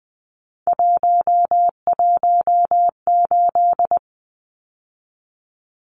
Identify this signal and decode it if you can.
Morse code, heard as a keyed tone, about 20 wpm, 700 Hz. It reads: 118
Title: Numbers - 3-Digit 20wpm